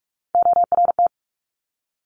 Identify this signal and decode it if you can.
Morse code, heard as a keyed tone, about 46 wpm, 690 Hz. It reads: ORT
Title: Top Three Letter Combinations in Words 45wpm